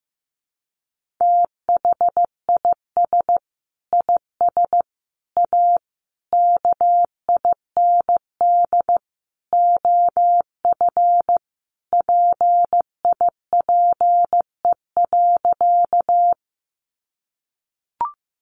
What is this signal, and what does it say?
Morse code, heard as a keyed tone, about 15 wpm, 700 Hz. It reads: THIS IS A KIND OF PIPE.
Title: Sentences from Top 1500 Words 15wpm